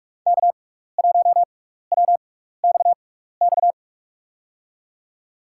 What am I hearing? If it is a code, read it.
Morse code, heard as a keyed tone, about 47 wpm, spaced as for 16 wpm, 700 Hz. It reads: K1WXX
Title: USA Call Signs - 5 Characters - 6x Character Spacing - 45wpm